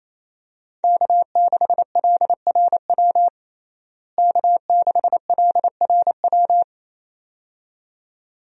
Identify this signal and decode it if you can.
Morse code, heard as a keyed tone, about 28 wpm, 700 Hz. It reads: K6LRW K6LRW
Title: USA Call Signs - 5 Characters - Repeated 2x - 28wpm